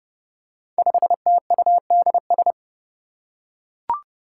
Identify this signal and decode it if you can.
Morse code, heard as a keyed tone, about 30 wpm, 700 Hz. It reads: 5TUDS
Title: Five Challenging Letter-Number Combinations 30wpm